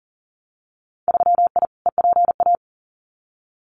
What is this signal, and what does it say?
Morse code, heard as a keyed tone, about 40 wpm, 700 Hz. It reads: 3I EPA
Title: Winter Field Day Exchanges Encore 40wpm